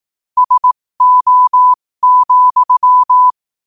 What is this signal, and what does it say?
Morse code, heard as a keyed tone, about 18 wpm, spaced as for 15 wpm, 1000 Hz. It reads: SO,
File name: qotdfiles / 15wpm